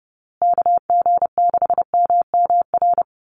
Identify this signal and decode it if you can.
Morse code, heard as a keyed tone, about 30 wpm, 700 Hz. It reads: KG6MMR
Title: USA Call Signs Repeated 3x 30wpm